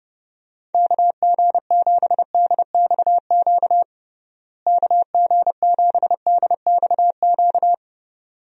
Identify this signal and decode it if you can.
Morse code, heard as a keyed tone, about 30 wpm, 700 Hz. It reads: KG7DXQ KG7DXQ
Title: USA Call Signs - 6 Characters - Repeated 2x - 30wpm